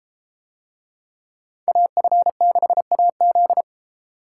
Morse code, heard as a keyed tone, about 31 wpm, 700 Hz. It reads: AF6AZ